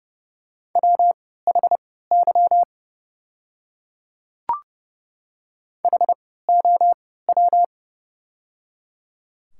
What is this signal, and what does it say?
Morse code, heard as a keyed tone, about 30 wpm, spaced as for 17 wpm, 700 Hz. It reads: WHY HOW
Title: 3x Character Spacing - Top 300 Words - 30wpm